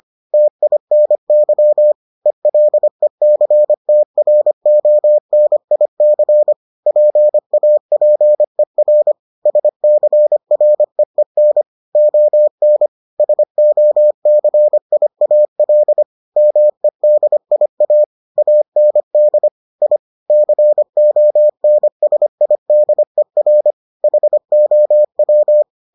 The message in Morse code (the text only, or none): TINY ELECTRONIC PAPER SCREEN ON SOCIAL MEDIA AND I CONSIDER HOW